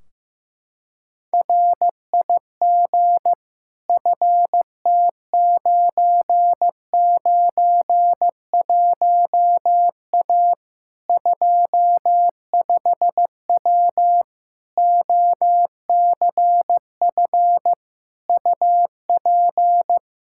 Morse code, heard as a keyed tone, about 15 wpm, 700 Hz. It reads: RIG FT991A 25W OCF UP
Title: Practice QSO - Exchange 2 - Info - 15wpm